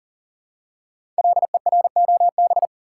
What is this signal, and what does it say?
Morse code, heard as a keyed tone, about 42 wpm, 690 Hz. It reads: LEROB
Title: Code Groups - 5 Letters - 40wpm